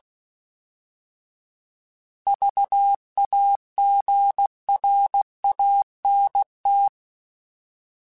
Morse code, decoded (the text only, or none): VAGRANT